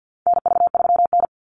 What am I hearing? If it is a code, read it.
Morse code, heard as a keyed tone, about 50 wpm, 700 Hz. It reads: NV3N
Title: Call Signs - 50wpm